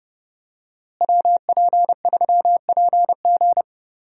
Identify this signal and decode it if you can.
Morse code, heard as a keyed tone, about 30 wpm, 700 Hz. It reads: WP3PG
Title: USA Call Signs - 5 Characters - Repeated 3x - 30wpm